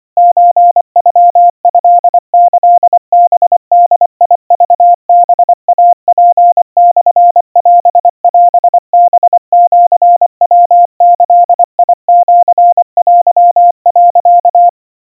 Morse code (Japanese Wabun code), heard as a keyed tone, about 24 wpm, 700 Hz. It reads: ソノトキハボクハイツモオオハシヤギシテ、
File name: 90 cpm 700 Hz Crispy